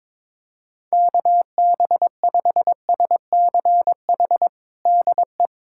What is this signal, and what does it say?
Morse code, heard as a keyed tone, about 22 wpm, 700 Hz. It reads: KB5SCH DE
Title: USA Callsign DE Callsign - 6 Characters - 22wpm